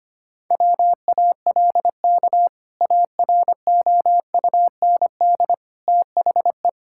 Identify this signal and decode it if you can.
Morse code, heard as a keyed tone, about 25 wpm, 700 Hz. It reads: WALK AROUND THE
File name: practice / 25wpm / Sentences with 2-Word Phrases - 25wpm